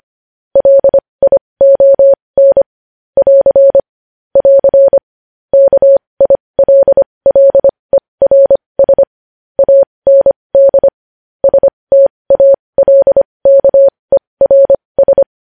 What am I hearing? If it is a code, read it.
Morse code, heard as a keyed tone, about 25 wpm, spaced as for 20 wpm, 550 Hz. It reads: LION <AR> <AR> KILLERS AND STALKERS
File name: podcast / News headlines-1x-25x20